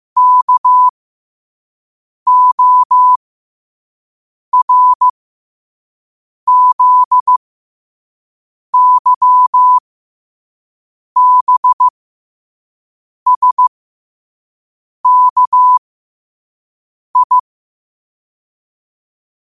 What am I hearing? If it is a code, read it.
Morse code, heard as a keyed tone, about 15 wpm, spaced as for 5 wpm, 1000 Hz. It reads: KORZYBSKI
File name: qotdfiles / 5wpm